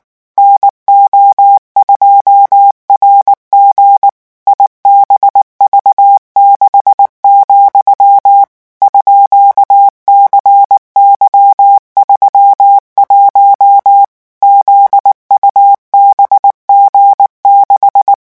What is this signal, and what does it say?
Morse code, heard as a keyed tone, about 19 wpm, 800 Hz. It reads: NO2RG IBV6, _CY31 ZUBG6
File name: morse mp3 / revision13-vitesse 18 mots